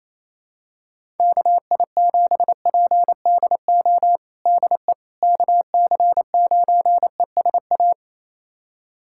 Morse code, heard as a keyed tone, about 28 wpm, 700 Hz. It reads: KI7PDO DE KC9ESA
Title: USA Callsign DE Callsign - 6 Characters - 28wpm